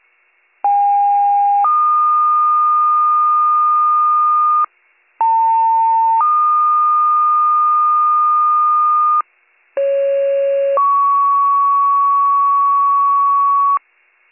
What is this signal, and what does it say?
Two-tone page, heard as a one-second tone, then a three-second tone, about 810 then 1220 Hz, about 900 then 1220 Hz, about 550 then 1120 Hz. It is heard from 0.6 to 13.8 s.